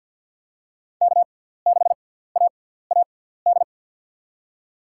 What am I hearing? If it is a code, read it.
Morse code, heard as a keyed tone, about 53 wpm, spaced as for 17 wpm, 700 Hz. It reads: K6AAD
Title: USA Call Signs - 6x Character Spacing - 50wpm